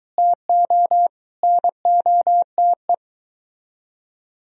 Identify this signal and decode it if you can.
Morse code, heard as a keyed tone, about 24 wpm, 700 Hz. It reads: TO NOTE